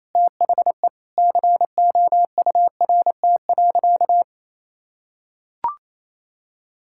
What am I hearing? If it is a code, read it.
Morse code, heard as a keyed tone, about 28 wpm, 700 Hz. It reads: THE COURT.
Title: Sentences from Top 1600 Words 28wpm